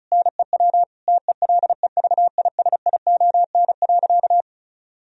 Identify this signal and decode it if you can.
Morse code, heard as a keyed tone, about 34 wpm, 700 Hz. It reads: NEW TELEVISION.
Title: Sentences from Top 1000 Words - Review Part 4 - 35wpm